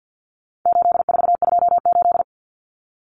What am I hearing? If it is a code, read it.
Morse code, heard as a keyed tone, about 54 wpm, 700 Hz. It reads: 8428
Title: Numbers - 4-Digit 50wpm